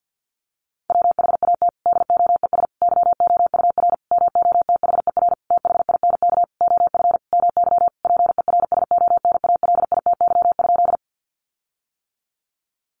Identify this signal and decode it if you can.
Morse code, heard as a keyed tone, about 50 wpm, 700 Hz. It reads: WHAT DOES YOUR MOTHER THINK OF MY PERSONALITY?
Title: Sentences from Top 1400 Words 50wpm